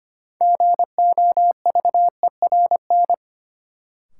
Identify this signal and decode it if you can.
Morse code, heard as a keyed tone, about 25 wpm, 700 Hz. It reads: GOVERN
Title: Sentences from Top 1000 Words - Review Part 2 - New Word Drill - 25wpm